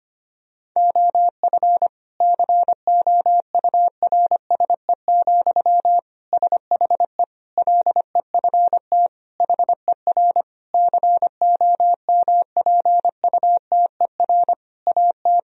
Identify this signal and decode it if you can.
Morse code, heard as a keyed tone, about 25 wpm, 700 Hz. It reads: OF COURSE, SHE LEFT HER COMPUTER AT
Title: Sentences from Top 1000 Words - Review Part 3 - 25wpm